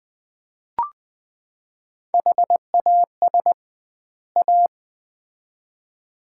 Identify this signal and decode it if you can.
Morse code, heard as a keyed tone, about 20 wpm, 700 Hz. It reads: HAS A
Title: 2x Word-Spacing - Sets of 2 Words - 20wpm